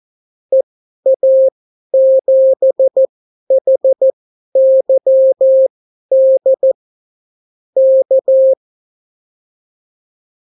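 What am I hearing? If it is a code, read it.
Morse code, heard as a keyed tone, about 14 wpm, spaced as for 11 wpm, 530 Hz. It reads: EA7HYD K
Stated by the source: Example Spanish CW QSO from a Morse learning book, 14 wpm characters with 8 wpm spacing.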